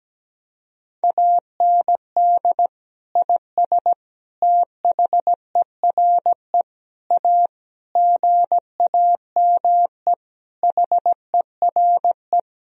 Morse code, heard as a keyed tone, about 17 wpm, 700 Hz. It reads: AND IS THERE A GAME HERE
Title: Sentences from Top 1000 Words - Review Part 3 - 17wpm